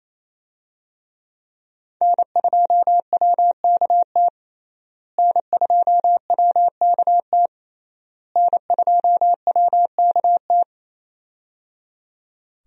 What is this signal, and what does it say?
Morse code, heard as a keyed tone, about 28 wpm, 700 Hz. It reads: N2WKT N2WKT N2WKT
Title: USA Call Signs - 5 Characters - Repeated 3x - 28wpm